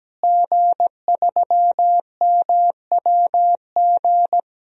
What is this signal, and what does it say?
Morse code, heard as a keyed tone, about 17 wpm, 700 Hz. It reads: G3MWG
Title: DXCC-100 Call Signs - 5 Characters - Repeated 3x - 17wpm